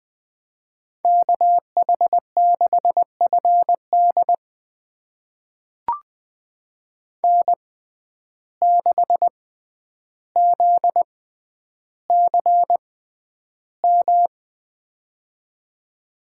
Morse code, heard as a keyed tone, about 20 wpm, spaced as for 10 wpm, 700 Hz. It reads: KH6FD N6ZCM
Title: USA Call Signs - 5 Characters - 6x Character Spacing - 20wpm